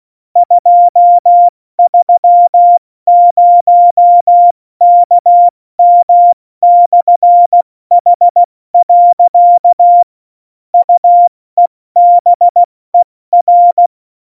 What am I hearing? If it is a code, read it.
Morse code, heard as a keyed tone, about 16 wpm, 700 Hz. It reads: 230KM/H. UEBER